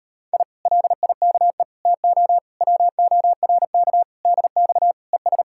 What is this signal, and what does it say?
Morse code, heard as a keyed tone, about 33 wpm, 700 Hz. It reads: I LIKE TO WORK DX ES